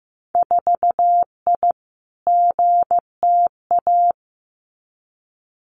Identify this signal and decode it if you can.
Morse code, heard as a keyed tone, about 15 wpm, 700 Hz. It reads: 4I GTA
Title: Winter Field Day Exchanges Encore 15wpm